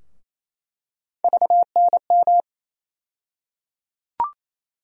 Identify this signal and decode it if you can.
Morse code, heard as a keyed tone, about 28 wpm, 700 Hz. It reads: VNM
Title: USA Call Sign Postfixes - 3 Letters 28wpm